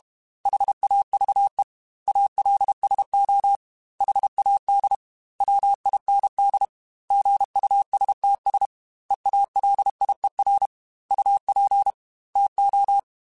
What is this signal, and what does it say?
Morse code, heard as a keyed tone, about 32 wpm, 800 Hz. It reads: HAVE ALSO HAD WIND GUSTS EALIER UP TO
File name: Weather Sets / Weather Sets-mp3s / Weather 5 5 @30